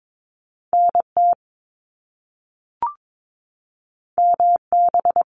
Morse code, heard as a keyed tone, about 22 wpm, 700 Hz. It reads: NT MB
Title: Canadian Provinces and Territories 22wpm